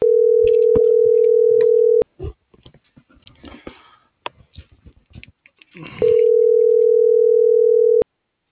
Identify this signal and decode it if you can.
Ringback tone (North America), 440 + 480 Hz.